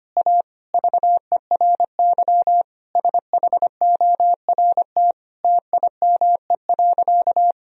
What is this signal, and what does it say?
Morse code, heard as a keyed tone, about 25 wpm, 700 Hz. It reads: A VERY SHORT TIME.
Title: Sentences from Top 1000 Words - Review Part 3 - 25wpm